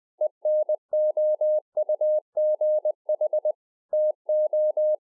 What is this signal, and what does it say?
Morse code, heard as a keyed tone, about 20 wpm, 610 Hz. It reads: ENOUGH TO
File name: 2011-04-mp3 / A SMALL TRANCEIVER 3 20WPM